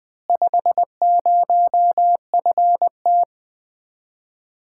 Morse code, heard as a keyed tone, about 20 wpm, 700 Hz. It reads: 50FT
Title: Common QSO Phrases - 4x Word-Spacing - 20wpm